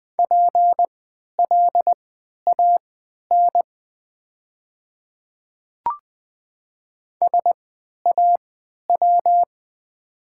Morse code, heard as a keyed tone, about 20 wpm, spaced as for 11 wpm, 700 Hz. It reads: PLAN SAW
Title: 3x Character Spacing - Top 500 Words - 20wpm